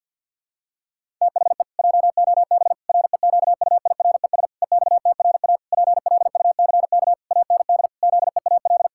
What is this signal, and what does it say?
Morse code, heard as a keyed tone, about 46 wpm, 700 Hz. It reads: THE JOB REQUIRES EXTRA PLUCK AND ZEAL
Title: Pangram Challenge v2 - 50wpm